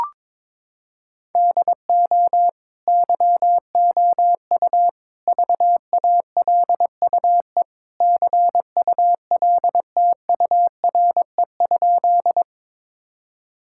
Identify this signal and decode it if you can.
Morse code, heard as a keyed tone, about 22 wpm, 700 Hz. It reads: DO YOU VALUE CULTURE?
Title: Sentences from Top 1000 Words - Review Part 6 - 22wpm